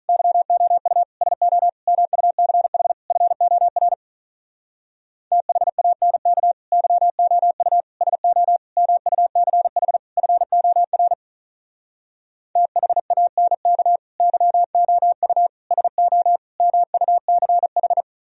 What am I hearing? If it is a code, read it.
Morse code, heard as a keyed tone, about 46 wpm, 700 Hz. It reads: YOU SO MUCH FOR THANK YOU SO MUCH FOR THANK YOU SO MUCH